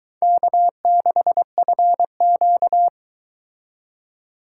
Morse code, heard as a keyed tone, about 23 wpm, 700 Hz. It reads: K6FQ